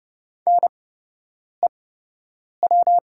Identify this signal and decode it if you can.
Morse code, heard as a keyed tone, about 30 wpm, spaced as for 8 wpm, 690 Hz. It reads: NEW
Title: 8x Character Spacing - Top 300 Words - 30wpm